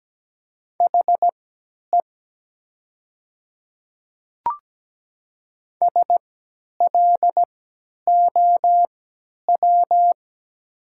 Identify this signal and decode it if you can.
Morse code, heard as a keyed tone, about 17 wpm, spaced as for 10 wpm, 700 Hz. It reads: HE SLOW
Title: 3x Character Spacing - Top 500 Words - 17wpm